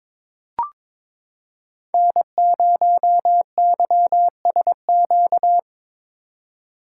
Morse code, heard as a keyed tone, about 22 wpm, 700 Hz. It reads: N0YSQ
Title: K1USN SST QSO Practice - 22wpm